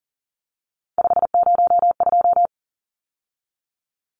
Morse code, heard as a keyed tone, about 40 wpm, 700 Hz. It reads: 502
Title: Numbers - 3-Digit 40wpm